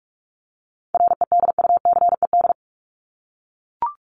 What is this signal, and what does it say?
Morse code, heard as a keyed tone, about 44 wpm, 700 Hz. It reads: REDUCED
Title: ED Suffix - Top 100 Words with Base Words 45wpm